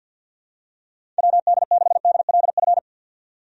Morse code, heard as a keyed tone, about 51 wpm, 700 Hz. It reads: WD6DLF
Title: USA Call Signs - 6 Characters - 50wpm